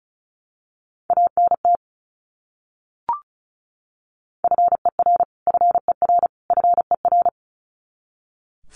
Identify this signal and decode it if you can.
Morse code, heard as a keyed tone, about 34 wpm, 700 Hz. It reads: ANT FER FER FER
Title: Abbreviations Repeated 3x - 35wpm